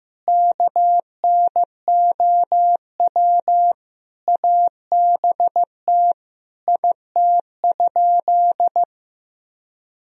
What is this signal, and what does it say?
Morse code, heard as a keyed tone, about 15 wpm, 700 Hz. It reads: KNOW ABT IT?
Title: Sentences from Top 100 Words and Abbreviations 15wpm